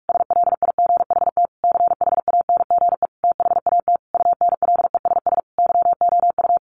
Morse code, heard as a keyed tone, about 44 wpm, 700 Hz. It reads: SLIGHT CHANGE THAT UNLESS YOU
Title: A Sound Of Thunder 45wpm